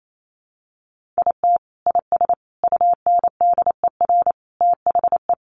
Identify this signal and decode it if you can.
Morse code, heard as a keyed tone, about 28 wpm, 700 Hz. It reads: IT IS UNDER THE
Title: Sentences from Top 1400 Words 28wpm